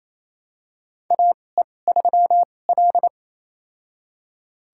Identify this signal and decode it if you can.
Morse code, heard as a keyed tone, about 27 wpm, spaced as for 20 wpm, 700 Hz. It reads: AE3L
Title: USA Call Signs - 4 Characters - 2x Character Spacing - 28wpm